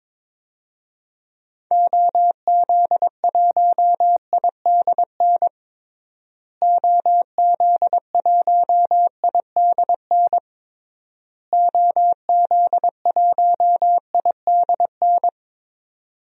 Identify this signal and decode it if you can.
Morse code, heard as a keyed tone, about 22 wpm, 700 Hz. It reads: OZ1IDN OZ1IDN OZ1IDN
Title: DXCC-100 Call Signs - 6 Characters - Repeated 3x - 22wpm